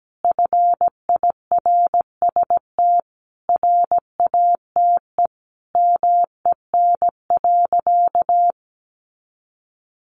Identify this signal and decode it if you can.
Morse code, heard as a keyed tone, about 17 wpm, 700 Hz. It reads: FIRST RATE MEN.
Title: Sentences from Top 1300 Words 17wpm